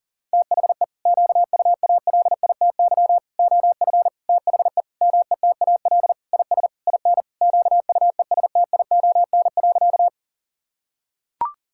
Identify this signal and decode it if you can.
Morse code, heard as a keyed tone, about 40 wpm, 700 Hz. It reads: THE QUALITY OF THE METAL IS IN QUESTION.
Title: Sentences from Top 1000 Words - Review Part 6 - 40wpm